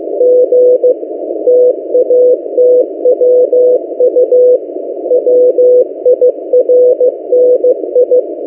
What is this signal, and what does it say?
Morse code, heard as a keyed tone, about 15 wpm, 500 Hz. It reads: G TATWU WIRNI
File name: M08a- -1355utc-8096khz